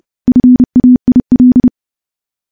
Morse code, heard as a keyed tone, about 30 wpm, 250 Hz. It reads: FAIL
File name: morse / morse-training-30WPM